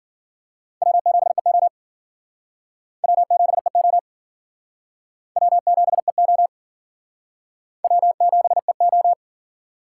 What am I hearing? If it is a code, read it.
Morse code, heard as a keyed tone, about 50 wpm, 700 Hz. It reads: W7EO W7EO W7EO W7EO
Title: Speed Racing - USA 1x2 Call Signs 40wpm